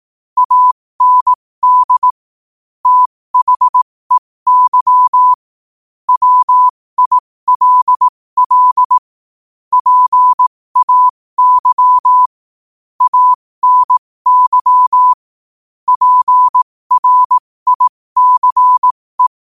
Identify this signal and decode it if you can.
Morse code, heard as a keyed tone, about 18 wpm, spaced as for 15 wpm, 1000 Hz. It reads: AND THEY WILL PAY ANY PRICE
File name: qotdfiles / 15wpm